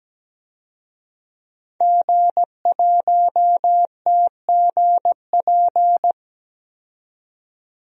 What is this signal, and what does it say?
Morse code, heard as a keyed tone, about 17 wpm, 700 Hz. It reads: G1TGP
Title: DXCC-100 Call Signs - 5 Characters - 17wpm